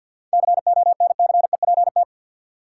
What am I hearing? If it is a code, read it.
Morse code, heard as a keyed tone, about 47 wpm, 700 Hz. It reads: CONCEPT
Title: Seven Letter Words - 50wpm